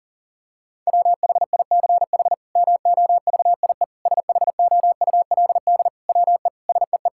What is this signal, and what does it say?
Morse code, heard as a keyed tone, about 40 wpm, 700 Hz. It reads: WHICH MOVIE SHOULD WE SEE
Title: Sentences from Top 1000 Words - Review Part 5 - 40wpm